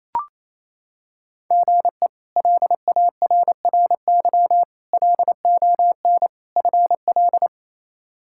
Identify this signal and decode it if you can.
Morse code, heard as a keyed tone, about 28 wpm, 700 Hz. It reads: GE LARRY LON FL
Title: K1USN SST QSO Practice - 28wpm